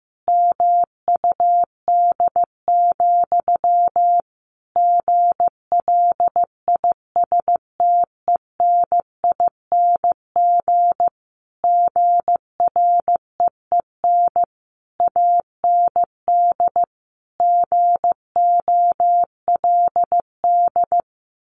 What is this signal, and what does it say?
Morse code, heard as a keyed tone, about 15 wpm, 700 Hz. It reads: MUD, GLISTENING GREEN AND GOLD
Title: A Sound Of Thunder 15wpm